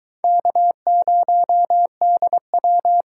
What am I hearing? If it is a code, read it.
Morse code, heard as a keyed tone, about 22 wpm, 700 Hz. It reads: K0DW